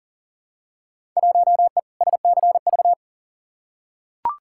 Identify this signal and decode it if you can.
Morse code, heard as a keyed tone, about 40 wpm, 700 Hz. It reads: 1E SCV
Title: Field Day Exchanges - 40wpm